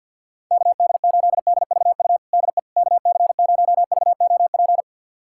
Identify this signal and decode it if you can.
Morse code, heard as a keyed tone, about 51 wpm, 700 Hz. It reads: KD8DVU DE KC0VOP
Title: USA Callsign DE Callsign 50wpm